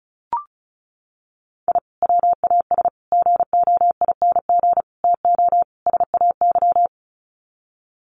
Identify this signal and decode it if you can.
Morse code, heard as a keyed tone, about 34 wpm, 700 Hz. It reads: I WAS GOING TO SAY
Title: Sets of 5 Words 35wpm